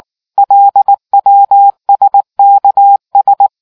Morse code, heard as a keyed tone, about 18 wpm, 800 Hz. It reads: LWSKS